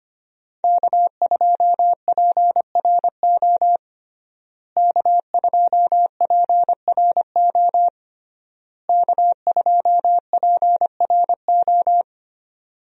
Morse code, heard as a keyed tone, about 25 wpm, 700 Hz. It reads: K2PRO K2PRO K2PRO
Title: USA Call Signs - 5 Characters - Repeated 3x - 25wpm